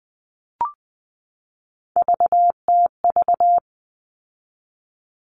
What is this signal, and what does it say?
Morse code, heard as a keyed tone, about 20 wpm, 700 Hz. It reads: VTV
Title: All Cut Numbers - 3-Digit 20wpm